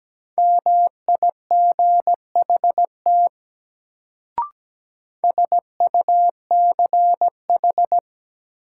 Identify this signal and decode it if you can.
Morse code, heard as a keyed tone, about 17 wpm, 700 Hz. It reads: MIGHT SUCH
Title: Sets of 2 Words - New Word Drill - 17wpm